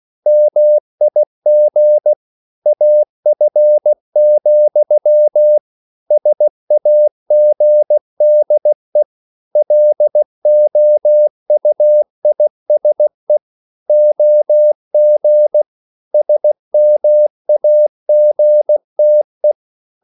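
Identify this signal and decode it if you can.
Morse code, heard as a keyed tone, about 16 wpm, 600 Hz. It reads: MIG AF, SAGDE LOUISE OG SMAGTE
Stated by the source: Slow (QRS) Morse code at 16 wpm, from CW med Gnister.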